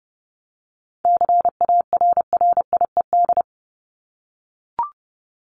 Morse code, heard as a keyed tone, about 30 wpm, 700 Hz. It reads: CARRIED
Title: ED Suffix - Top 100 Words with Base Words 30wpm